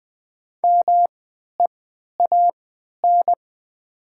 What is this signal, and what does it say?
Morse code, heard as a keyed tone, about 20 wpm, spaced as for 11 wpm, 700 Hz. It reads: MEAN
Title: 3x Character Spacing - Top 300 Words - 20wpm